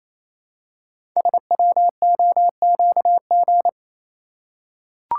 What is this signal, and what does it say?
Morse code, heard as a keyed tone, about 28 wpm, 700 Hz. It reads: SWOQG
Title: Code Groups - 5 Letters - 28wpm